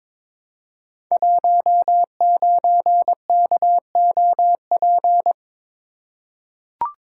Morse code, heard as a keyed tone, about 22 wpm, 690 Hz. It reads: 19KOP
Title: Code Groups - 5 Letter:Numbers - 22wpm